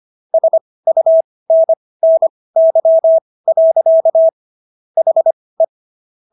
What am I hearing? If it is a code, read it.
Morse code, heard as a keyed tone, about 25 wpm, spaced as for 18 wpm, 650 Hz. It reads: SUNNY. HE